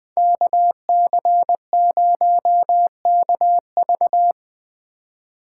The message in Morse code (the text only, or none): KC0KV